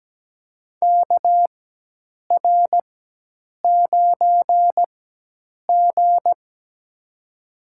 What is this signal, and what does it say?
Morse code, heard as a keyed tone, about 17 wpm, spaced as for 8 wpm, 700 Hz. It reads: KR9G
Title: USA Call Signs - 4 Characters - 4x Character Spacing - 17wpm